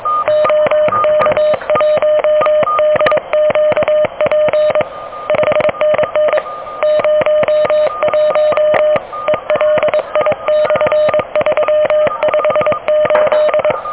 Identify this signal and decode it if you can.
Morse code, heard as a keyed tone, about 20 wpm, 600 Hz. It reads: OK1DQP 5NN 01 ELI/358
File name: OK QRP 2023 / OK QRP 2023